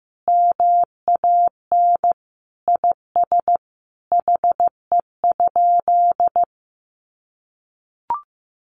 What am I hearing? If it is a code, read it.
Morse code, heard as a keyed tone, about 15 wpm, 700 Hz. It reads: MAN IS HE?
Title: Sentences from Top 1300 Words 15wpm